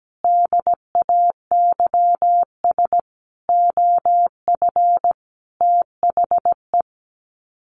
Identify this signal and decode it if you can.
Morse code, heard as a keyed tone, about 17 wpm, 700 Hz. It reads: DAYS OF THE
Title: Sets of 5 Words Encore 17wpm